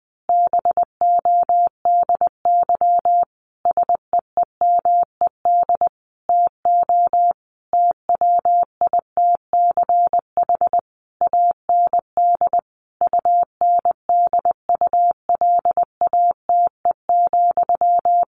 Morse code, heard as a keyed tone, about 20 wpm, 700 Hz. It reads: BODY SEEMED TO TWITCH AND UNDULATE,
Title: A Sound Of Thunder 20wpm